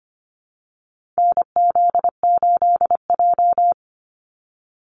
Morse code, heard as a keyed tone, about 25 wpm, 700 Hz. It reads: NZ8J
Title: Call Signs Repeated 2x - 25wpm